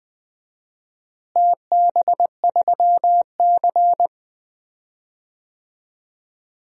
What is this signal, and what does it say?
Morse code, heard as a keyed tone, about 20 wpm, 700 Hz. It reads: TB3C
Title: DXCC-100 Call Signs - 4 Characters - Repeated 2x - 20wpm